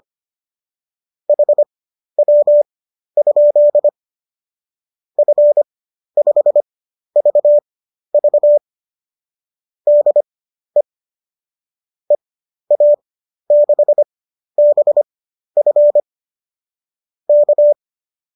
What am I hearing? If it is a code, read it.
Morse code, heard as a keyed tone, about 25 wpm, spaced as for 12 wpm, 570 Hz. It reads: HW? F5VV DE EA6BF K
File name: qso009 12